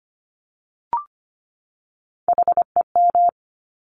Morse code, heard as a keyed tone, about 25 wpm, 700 Hz. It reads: HEM
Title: Top Three Letter Combinations in Words 25wpm